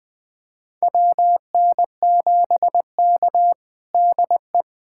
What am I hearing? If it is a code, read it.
Morse code, heard as a keyed tone, about 20 wpm, 700 Hz. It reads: WN7K DE
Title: USA Callsign DE Callsign - 4 Characters - 20wpm